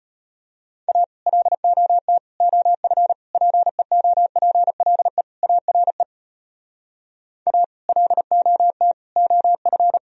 Morse code, heard as a keyed tone, about 33 wpm, 700 Hz. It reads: A LOT OF PEOPLE ARE A LOT OF